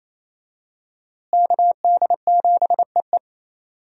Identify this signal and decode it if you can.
Morse code, heard as a keyed tone, about 28 wpm, 700 Hz. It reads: KD7EE
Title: USA Call Signs - 5 Characters - Repeated 2x - 28wpm